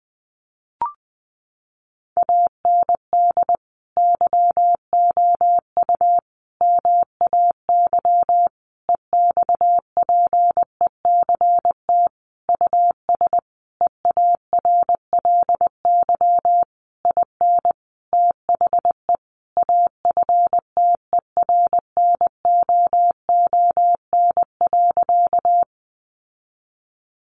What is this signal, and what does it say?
Morse code, heard as a keyed tone, about 20 wpm, 700 Hz. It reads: AND YOU MAY EXPECT US EARLY IN THE AFTERNOON.
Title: Sentences from Top 1500 Words 20wpm